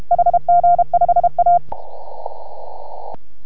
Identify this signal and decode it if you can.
Morse code, heard as a keyed tone, about 33 wpm, 690 Hz. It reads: HG5A